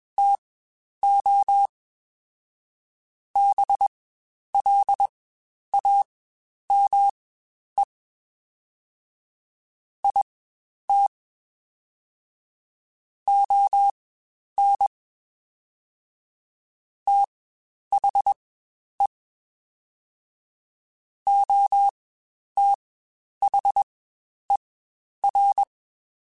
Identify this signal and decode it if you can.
Morse code, heard as a keyed tone, about 21 wpm, spaced as for 10 wpm, 800 Hz. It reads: TO BLAME IT ON THE OTHER
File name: Spaced / Chuckles Spaced-mp3s / HumanErr sp @20